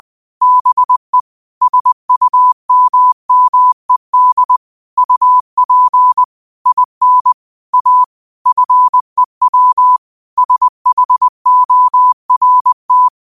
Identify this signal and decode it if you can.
Morse code, heard as a keyed tone, about 20 wpm, 1000 Hz. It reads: BE SUMMED UP IN A FEW SHORT
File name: qotdfiles / 20wpm